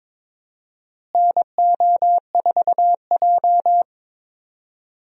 Morse code, heard as a keyed tone, about 22 wpm, 700 Hz. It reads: NO4J
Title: USA Call Signs - 4 Characters - Repeated 3x - 22wpm